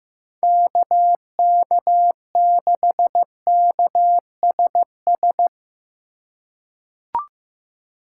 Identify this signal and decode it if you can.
Morse code, heard as a keyed tone, about 16 wpm, 700 Hz. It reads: KK6KSS
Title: USA Call Signs - 4x Character Spacing - 15wpm